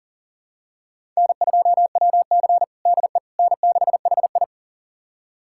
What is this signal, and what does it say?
Morse code, heard as a keyed tone, about 37 wpm, 700 Hz. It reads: N2WC DE N6HI
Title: USA Callsign DE Callsign - 4 Characters - 40wpm